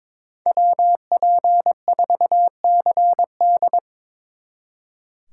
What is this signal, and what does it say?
Morse code, heard as a keyed tone, about 24 wpm, 700 Hz. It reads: WP4CD
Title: Speed Racing - USA 2x2 Call Signs 22wpm